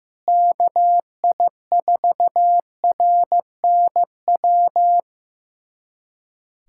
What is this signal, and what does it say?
Morse code, heard as a keyed tone, about 15 wpm, 700 Hz. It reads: KI4RNW
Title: USA Callsign DE Callsign - 6 Characters - 15wpm